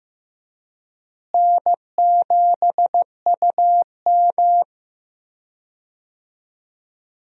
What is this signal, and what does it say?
Morse code, heard as a keyed tone, about 15 wpm, 700 Hz. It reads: N7UM
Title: USA Call Signs - 4 Characters - Repeated 3x - 15wpm